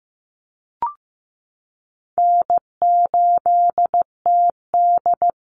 Morse code, heard as a keyed tone, about 15 wpm, 700 Hz. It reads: N8TD
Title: Easy Call Signs Repeated 3x - 15wpm